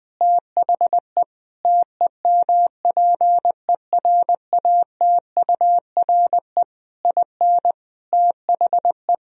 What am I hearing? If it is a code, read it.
Morse code, heard as a keyed tone, about 21 wpm, 700 Hz. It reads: THE TEMPERATURE IN THE
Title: Sentences from Top 1000 Words - Review Part 4 - 20wpm